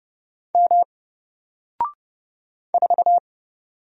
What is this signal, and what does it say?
Morse code, heard as a keyed tone, about 30 wpm, 700 Hz. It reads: M 4
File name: Single-Letter-Number-30wpm